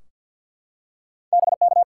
Morse code, heard as a keyed tone, about 53 wpm, 700 Hz. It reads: BK
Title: Top 180 QSO Elements 50wpm